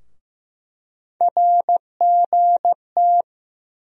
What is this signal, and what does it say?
Morse code, heard as a keyed tone, about 15 wpm, 700 Hz. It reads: RGT
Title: USA Call Sign Postfixes - 3 Letters 15wpm